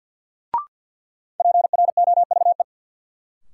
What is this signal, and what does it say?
Morse code, heard as a keyed tone, about 51 wpm, 700 Hz. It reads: PROVE
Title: Sentences from Top 1600 Words - New Word Drill - 50wpm